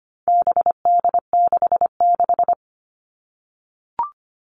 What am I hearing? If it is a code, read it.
Morse code, heard as a keyed tone, about 25 wpm, 700 Hz. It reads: BD66
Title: All Cut Numbers - 4-Digit 25wpm